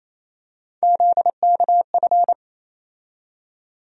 Morse code, heard as a keyed tone, about 28 wpm, 700 Hz. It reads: ZKF
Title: USA Call Sign Postfixes - 3 Letters 28wpm